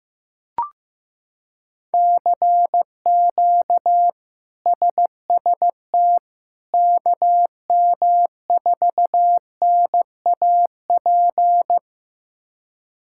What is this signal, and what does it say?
Morse code, heard as a keyed tone, about 15 wpm, 700 Hz. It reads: CQ SST KM4NAP
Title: K1USN SST QSO Practice - 15wpm